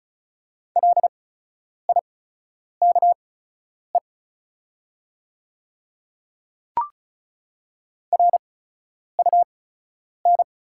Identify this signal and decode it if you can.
Morse code, heard as a keyed tone, about 36 wpm, spaced as for 10 wpm, 690 Hz. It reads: LIKE RUN
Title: 8x Character Spacing - Top 300 Words - 35wpm